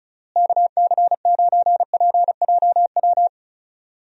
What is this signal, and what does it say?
Morse code, heard as a keyed tone, about 36 wpm, 700 Hz. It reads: KC9PJW
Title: USA Call Signs - 6 Characters - Repeated 3x - 35wpm